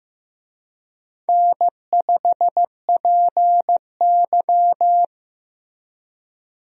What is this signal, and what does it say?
Morse code, heard as a keyed tone, about 15 wpm, 700 Hz. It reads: N5PY
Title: USA Call Signs - 4 Characters - Repeated 3x - 15wpm